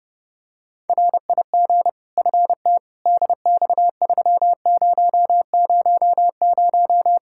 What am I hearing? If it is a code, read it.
Morse code, heard as a keyed tone, about 32 wpm, 700 Hz. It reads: RIG FT DX3000
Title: Practice QSO - Exchange 2 - Info - 30wpm